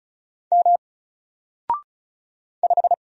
Morse code, heard as a keyed tone, about 36 wpm, 700 Hz. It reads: M 5
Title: Single Letter-Number 35wpm